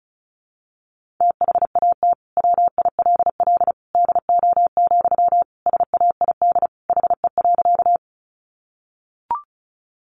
Morse code, heard as a keyed tone, about 34 wpm, 700 Hz. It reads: THAT WILL DO, SAID HE.
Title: Sentences from Top 100 Words 35wpm